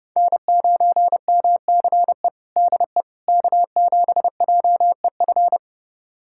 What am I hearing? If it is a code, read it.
Morse code, heard as a keyed tone, about 32 wpm, 700 Hz. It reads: N9MCE DE K7JEF
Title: USA Callsign DE Callsign - 5 Characters - 30wpm